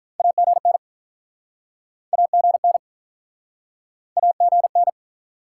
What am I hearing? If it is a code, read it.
Morse code, heard as a keyed tone, about 50 wpm, 700 Hz. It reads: AGN AGN AGN